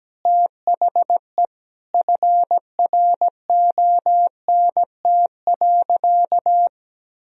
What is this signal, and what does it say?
Morse code, heard as a keyed tone, about 17 wpm, 700 Hz. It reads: THE FRONT.
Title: Sentences from Top 1000 Words - Review Part 2 - 17wpm